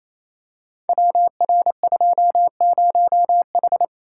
Morse code, heard as a keyed tone, about 27 wpm, 700 Hz. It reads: WR20H
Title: Code Groups - 5 Letter:Numbers - 28wpm